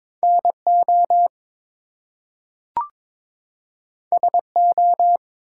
Morse code, heard as a keyed tone, about 22 wpm, 700 Hz. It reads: NO SO
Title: Two Letter Words - 22wpm